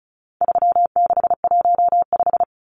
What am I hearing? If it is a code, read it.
Morse code, heard as a keyed tone, about 34 wpm, 700 Hz. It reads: 3615
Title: Numbers - 4-Digit 35wpm